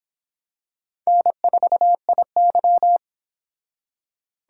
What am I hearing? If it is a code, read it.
Morse code, heard as a keyed tone, about 26 wpm, 700 Hz. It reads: N4IY